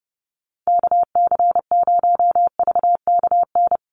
Morse code, heard as a keyed tone, about 30 wpm, 700 Hz. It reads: KC0VKN
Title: Call Signs Repeated 2x - 30wpm